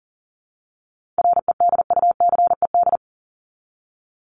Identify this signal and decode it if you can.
Morse code, heard as a keyed tone, about 40 wpm, 700 Hz. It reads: REDUCED
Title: ED Suffix - Top 100 Words with Base Words 40wpm